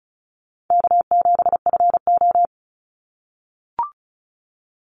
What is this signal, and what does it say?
Morse code, heard as a keyed tone, about 36 wpm, 700 Hz. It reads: K7FO
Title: NA SOTA Chaser Callsigns 35wpm